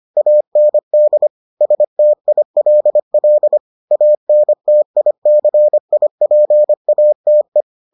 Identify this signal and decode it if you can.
Morse code, heard as a keyed tone, about 25 wpm, 600 Hz. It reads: AND STILL ANTICIPATE